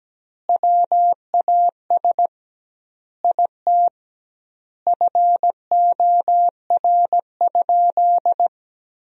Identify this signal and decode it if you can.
Morse code, heard as a keyed tone, about 17 wpm, 700 Hz. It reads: WAS IT FOR?
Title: 2x Word-Spacing - Sentences from Top 100 Words - 17wpm